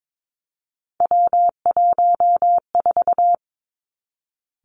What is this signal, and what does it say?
Morse code, heard as a keyed tone, about 22 wpm, 700 Hz. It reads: W14
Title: Three Challenging Letter-Number Combinations 22wpm